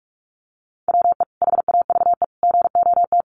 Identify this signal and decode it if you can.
Morse code, heard as a keyed tone, about 44 wpm, 700 Hz. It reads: WE HAVE GOT
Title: Sentences from Top 1200 Words 45wpm